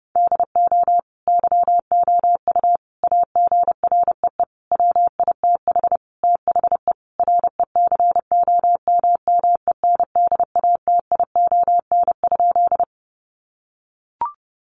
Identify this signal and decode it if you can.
Morse code, heard as a keyed tone, about 30 wpm, 700 Hz. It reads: DO YOU AGREE WITH THE RECOMMENDATION?
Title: Sentences from Top 1400 Words 30wpm